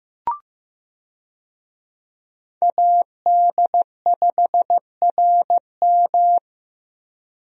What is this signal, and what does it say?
Morse code, heard as a keyed tone, about 15 wpm, 700 Hz. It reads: AD5RM
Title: USA Call Signs - 5 Characters - Repeated 2x - 15wpm